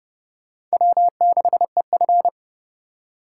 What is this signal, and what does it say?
Morse code, heard as a keyed tone, about 36 wpm, 700 Hz. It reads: W6EF
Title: Speed Racing - USA 1x2 Call Signs 30wpm